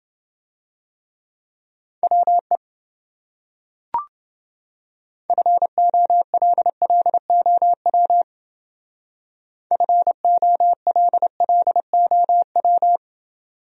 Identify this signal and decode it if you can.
Morse code, heard as a keyed tone, about 30 wpm, 700 Hz. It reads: WE FOLLOW FOLLOW